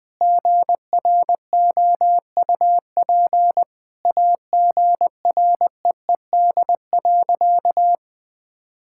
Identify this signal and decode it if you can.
Morse code, heard as a keyed tone, about 20 wpm, 700 Hz. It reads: GROUP AGREED.
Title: Sentences with 3-Word Phrases - 20wpm